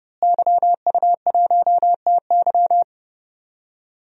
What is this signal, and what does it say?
Morse code, heard as a keyed tone, about 30 wpm, 700 Hz. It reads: YU1TY
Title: DXCC-100 Call Signs - 5 Characters - Repeated 2x - 30wpm